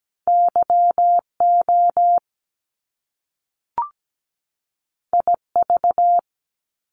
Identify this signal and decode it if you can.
Morse code, heard as a keyed tone, about 17 wpm, 700 Hz. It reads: YO IV
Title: Top Two Letter Combinations in Words 17wpm